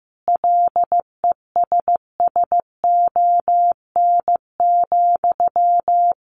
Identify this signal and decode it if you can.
Morse code, heard as a keyed tone, about 15 wpm, 700 Hz. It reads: LESSON,
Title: Sentences from Top 1200 Words 15wpm